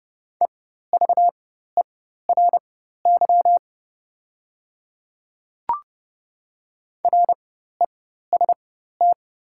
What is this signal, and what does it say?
Morse code, heard as a keyed tone, about 30 wpm, spaced as for 14 wpm, 700 Hz. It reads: EVERY REST
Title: 4x Character Spacing - Top 500 Words - 30wpm